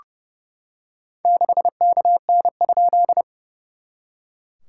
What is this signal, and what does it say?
Morse code, heard as a keyed tone, about 30 wpm, 690 Hz. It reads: 6KN?
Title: SOTA QSO Practice 30wpm